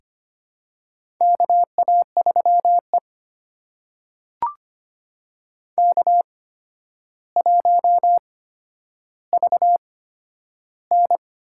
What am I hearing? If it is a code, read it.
Morse code, heard as a keyed tone, about 25 wpm, spaced as for 11 wpm, 700 Hz. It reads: KA3E K1VN
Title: USA Call Signs - 4 Characters - 8x Character Spacing - 25wpm